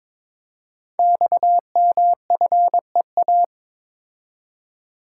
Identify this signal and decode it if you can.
Morse code, heard as a keyed tone, about 22 wpm, 700 Hz. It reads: XMFEA
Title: Code Groups - 5 Letter:Numbers - 22wpm